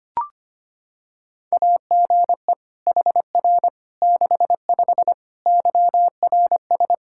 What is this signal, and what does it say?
Morse code, heard as a keyed tone, about 25 wpm, 700 Hz. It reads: AGE HR 65 YRS
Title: Practice QSO - Exchange 3 - More Info - 25wpm